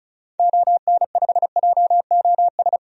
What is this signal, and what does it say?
Morse code, heard as a keyed tone, about 34 wpm, 700 Hz. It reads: ON5JOS
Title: DXCC-100 Call Signs - 6 Characters - Repeated 2x - 35wpm